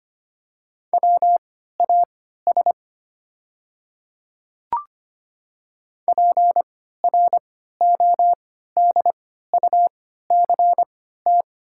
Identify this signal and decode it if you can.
Morse code, heard as a keyed tone, about 25 wpm, spaced as for 14 wpm, 700 Hz. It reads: WAS PRODUCT
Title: 3x Character Spacing - Top 500 Words - 25wpm